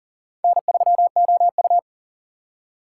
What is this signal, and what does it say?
Morse code, heard as a keyed tone, about 40 wpm, 700 Hz. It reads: N3OU
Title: USA Call Signs - 3x Character Spacing - 40wpm